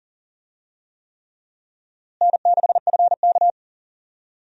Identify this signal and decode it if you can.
Morse code, heard as a keyed tone, about 40 wpm, 700 Hz. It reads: N6FK
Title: USA Call Signs - 4 Characters - Repeated 3x - 40wpm